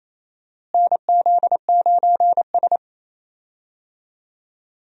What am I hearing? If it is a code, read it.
Morse code, heard as a keyed tone, about 28 wpm, 700 Hz. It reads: NZ9S
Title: USA Call Signs - 4 Characters - Repeated 3x - 28wpm